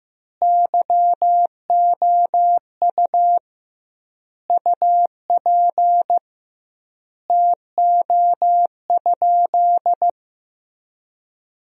Morse code, heard as a keyed tone, about 16 wpm, 700 Hz. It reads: YOU UP TO?
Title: 2x Word-Spacing - Sentences from Top 100 Words - 15wpm